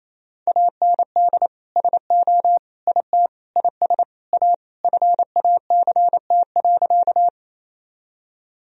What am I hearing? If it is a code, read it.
Morse code, heard as a keyed tone, about 28 wpm, 700 Hz. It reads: AND SO IT IS A FACT.
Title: Sentences from Top 1000 Words - Review Part 3 - 28wpm